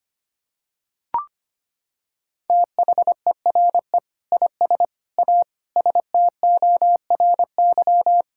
Morse code, heard as a keyed tone, about 25 wpm, 700 Hz. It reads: THERE IS A STORY
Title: Sentences from Top 1000 Words - Review Part 5 - 25wpm